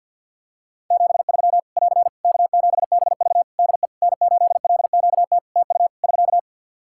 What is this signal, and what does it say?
Morse code, heard as a keyed tone, about 54 wpm, 700 Hz. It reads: 73 <AR> K7BV DE N8LQT TU <SK>
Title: Practice QSO - Exchange 4 - Ending - 50wpm